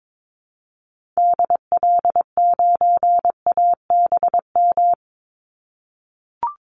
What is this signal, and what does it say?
Morse code, heard as a keyed tone, about 22 wpm, 700 Hz. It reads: DL9ABM
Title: Call Signs - 22wpm